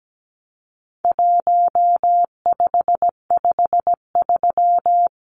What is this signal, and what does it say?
Morse code, heard as a keyed tone, about 17 wpm, 700 Hz. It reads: 1553
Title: Numbers - 4-Digit 17wpm